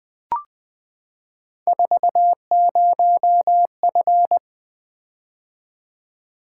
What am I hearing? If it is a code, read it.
Morse code, heard as a keyed tone, about 20 wpm, 700 Hz. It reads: 40F
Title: Three Letter-Number Combinations - 20wpm